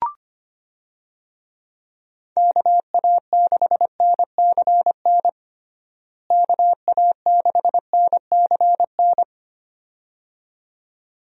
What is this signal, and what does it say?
Morse code, heard as a keyed tone, about 25 wpm, 700 Hz. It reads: KA6NCN KA6NCN
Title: USA Call Signs - 6 Characters - Repeated 2x - 25wpm